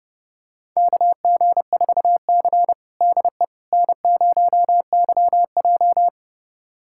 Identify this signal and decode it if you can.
Morse code, heard as a keyed tone, about 32 wpm, 700 Hz. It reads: KG4C DE N0YJ
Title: USA Callsign DE Callsign - 4 Characters - 30wpm